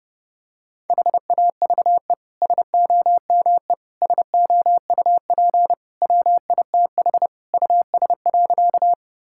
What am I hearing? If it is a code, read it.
Morse code, heard as a keyed tone, about 30 wpm, 700 Hz. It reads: HAVE SOME SOUP WITH US.
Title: Sentences from Top 1000 Words - Review Part 6 - 30wpm